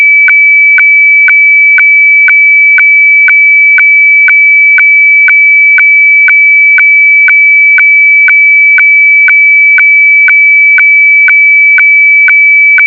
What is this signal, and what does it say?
Weather fax (radiofax).